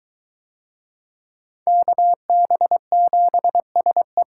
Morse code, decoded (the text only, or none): KB7SE